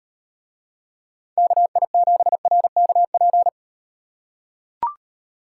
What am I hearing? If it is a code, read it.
Morse code, heard as a keyed tone, about 36 wpm, 700 Hz. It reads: KI7RKP